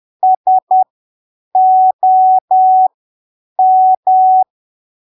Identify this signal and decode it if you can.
Morse code, heard as a keyed tone, about 10 wpm, spaced as for 7 wpm, 750 Hz. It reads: SOM
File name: arquivos / texto / texto 29 10ppm